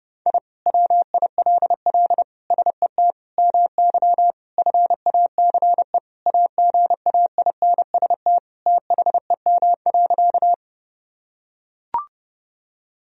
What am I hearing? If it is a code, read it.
Morse code, heard as a keyed tone, about 32 wpm, 700 Hz. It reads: I WILL SET MY FACE AGAINST THEM.
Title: Sentences from Top 1000 Words - Review Part 2 - 30wpm